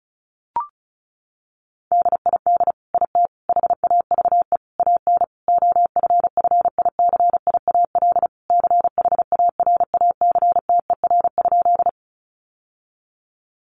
Morse code, heard as a keyed tone, about 34 wpm, 700 Hz. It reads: DID IT HAVE AN OFFICIAL CHARACTER?
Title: Sentences from Top 1200 Words 35wpm